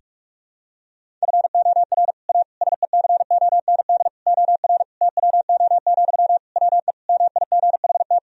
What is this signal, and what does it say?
Morse code, heard as a keyed tone, about 41 wpm, 700 Hz. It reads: FOR A SECOND OR TWO, WE MIGHT
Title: Sentences from Top 1000 Words - Review Part 1 - 45wpm